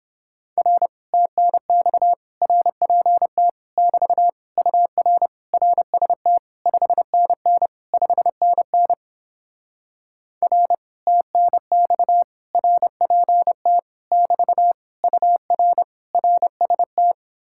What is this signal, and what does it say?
Morse code, heard as a keyed tone, about 31 wpm, 700 Hz. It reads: R TNX RPT <BT> UR RST 5NN 5NN R TNX RPT <BT> UR RST